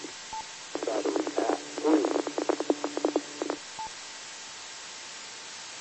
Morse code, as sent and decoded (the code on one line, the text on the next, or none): none